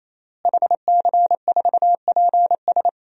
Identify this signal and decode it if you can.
Morse code, heard as a keyed tone, about 28 wpm, 700 Hz. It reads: HC4PS
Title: DXCC-100 Call Signs - 5 Characters - Repeated 3x - 28wpm